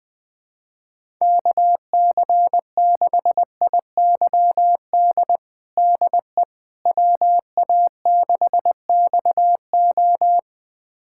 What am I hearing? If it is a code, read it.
Morse code, heard as a keyed tone, about 20 wpm, 700 Hz. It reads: KC6IYD DE WA6XO
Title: USA Callsign DE Callsign 20wpm